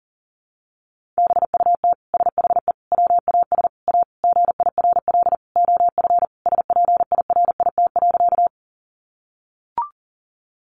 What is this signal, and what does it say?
Morse code, heard as a keyed tone, about 40 wpm, 700 Hz. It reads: BUT SHE WAS A GIRL OF SPIRIT.
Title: Sentences from Top 1400 Words 40wpm